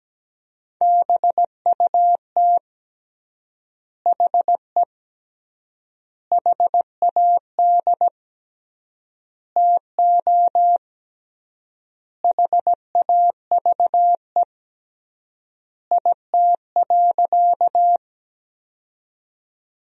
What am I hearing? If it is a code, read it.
Morse code, heard as a keyed tone, about 17 wpm, 700 Hz. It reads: BUT HE HAD TO HAVE IT.
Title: 3x Word-Spacing - Sentences from Top 100 Words - 17wpm